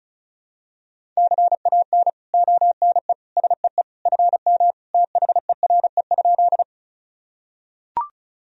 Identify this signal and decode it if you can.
Morse code, heard as a keyed tone, about 36 wpm, 700 Hz. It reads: CAN ONE SEE FM THERE?
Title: Sentences from Top 100 Words and Abbreviations 35wpm